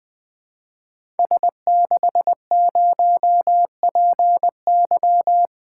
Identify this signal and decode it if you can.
Morse code, heard as a keyed tone, about 20 wpm, 700 Hz. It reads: S60PY
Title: Code Groups - 5 Letter:Numbers - 20wpm